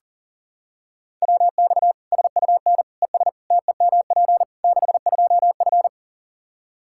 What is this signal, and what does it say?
Morse code, heard as a keyed tone, about 40 wpm, 700 Hz. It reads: WX SUN ES TEMP 62F
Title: Practice QSO - Exchange 2 - Info - 40wpm